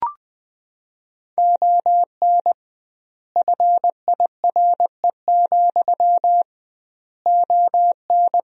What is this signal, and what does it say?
Morse code, heard as a keyed tone, about 20 wpm, 700 Hz. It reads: ON FIRE, ON
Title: 2x Word-Spacing - Sentences from Top 400 Words - 20wpm